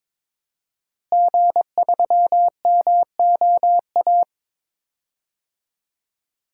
Morse code, heard as a keyed tone, about 22 wpm, 700 Hz. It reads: G3MOA
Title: DXCC-100 Call Signs - 5 Characters - Repeated 3x - 22wpm